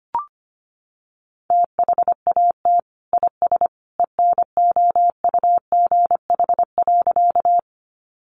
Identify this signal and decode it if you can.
Morse code, heard as a keyed tone, about 25 wpm, 700 Hz. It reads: THAT IS ENOUGH.
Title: Sentences from Top 400 Words 25wpm